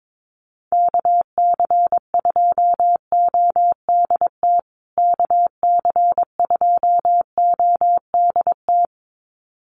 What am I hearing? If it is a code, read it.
Morse code, heard as a keyed tone, about 22 wpm, 700 Hz. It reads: KC2ODT KC2ODT
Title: USA Call Signs Repeated 2x 22wpm